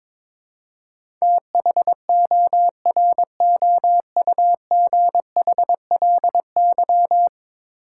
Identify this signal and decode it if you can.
Morse code, heard as a keyed tone, about 21 wpm, 700 Hz. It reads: THOROUGHLY
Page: morse-code-ninja-practice-22wpm